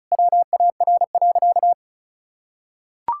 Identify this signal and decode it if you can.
Morse code, heard as a keyed tone, about 36 wpm, 700 Hz. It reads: WAR.
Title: Sentences from Top 1000 Words - Review Part 6 - 35wpm